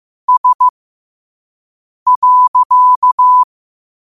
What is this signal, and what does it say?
Morse code, heard as a keyed tone, about 15 wpm, spaced as for 5 wpm, 1000 Hz. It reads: S.
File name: qotdfiles / 5wpm